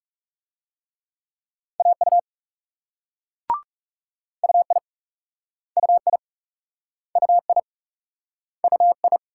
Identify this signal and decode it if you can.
Morse code, heard as a keyed tone, about 41 wpm, 700 Hz. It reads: AU UI UI UI UI